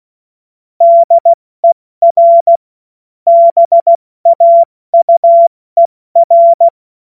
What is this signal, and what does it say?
Morse code, heard as a keyed tone, about 16 wpm, 680 Hz. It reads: DER BAUER
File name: Morsekurs / Witze / Witz 13. April 2021, 70 F80